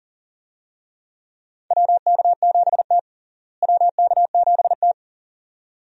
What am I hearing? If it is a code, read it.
Morse code, heard as a keyed tone, about 40 wpm, 700 Hz. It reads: WK7T WK7T
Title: USA Call Signs - 4 Characters - Repeated 2x - 40wpm